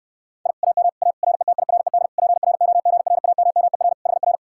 Morse code, heard as a keyed tone, about 68 wpm, 700 Hz. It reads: I AM A RETIRED PROGRAMMER HR